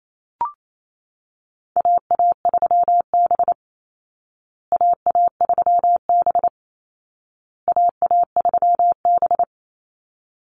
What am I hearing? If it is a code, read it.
Morse code, heard as a keyed tone, about 28 wpm, 700 Hz. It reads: AA3B AA3B AA3B
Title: Call Signs Repeated 2x - 28wpm